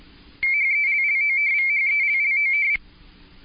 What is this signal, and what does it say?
RTTY, about 45.45 baud. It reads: HELLOVHELLOV